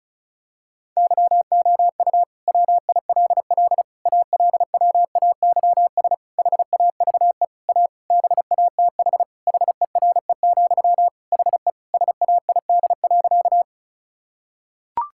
Morse code, heard as a keyed tone, about 34 wpm, 700 Hz. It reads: YOU WILL ALWAYS HAVE A BATH HERE, HE SAID.
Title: Sentences from Top 1600 Words 35wpm